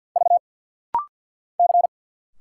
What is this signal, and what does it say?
Morse code, heard as a keyed tone, about 50 wpm, 700 Hz. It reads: V C
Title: Single Letter-Number Alt - Rapid-Fire 50wpm